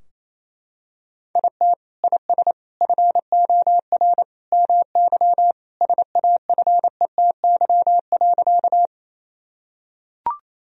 Morse code, heard as a keyed tone, about 28 wpm, 700 Hz. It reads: IT IS FOR MY SAFETY.
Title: Sentences from Top 1000 Words - Review Part 4 - 28wpm